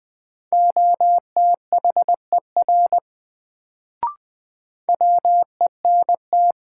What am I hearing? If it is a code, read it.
Morse code, heard as a keyed tone, about 20 wpm, 700 Hz. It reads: OTHER WENT
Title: Sets of 2 Words - New Word Drill - 20wpm